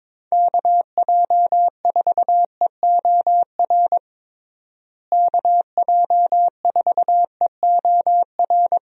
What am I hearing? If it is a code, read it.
Morse code, heard as a keyed tone, about 22 wpm, 700 Hz. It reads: KJ4EOR KJ4EOR
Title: USA Call Signs - 6 Characters - Repeated 2x - 22wpm